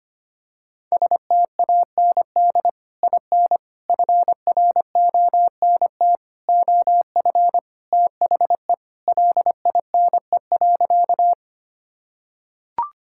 Morse code, heard as a keyed tone, about 25 wpm, 700 Hz. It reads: STAND IN FRONT OF THE LINE.
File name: practice / 25wpm / Sentences with 3-Word Phrases - 25wpm